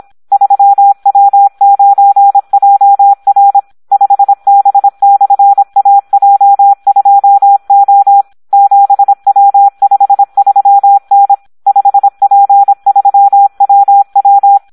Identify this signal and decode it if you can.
Morse code, heard as a keyed tone, about 26 wpm, 800 Hz. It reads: 3W9JR 5B/AJ2O 7W53N 5P3WW
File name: Audio practica CW / SD 1 26